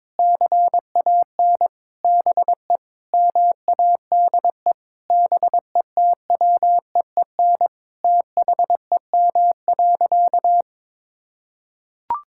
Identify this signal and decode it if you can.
Morse code, heard as a keyed tone, about 22 wpm, 700 Hz. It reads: CAN BE MADE BETWEEN THEM.
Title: Sentences from Top 1000 Words - Review Part 6 - 22wpm